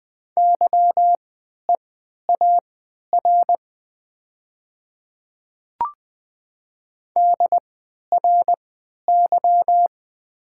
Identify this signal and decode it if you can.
Morse code, heard as a keyed tone, about 20 wpm, spaced as for 11 wpm, 700 Hz. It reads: YEAR DRY
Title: 3x Character Spacing - Top 500 Words - 20wpm